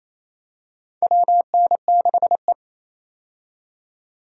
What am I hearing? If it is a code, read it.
Morse code, heard as a keyed tone, about 28 wpm, 700 Hz. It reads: WN6E
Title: USA Call Signs - 4 Characters - 28wpm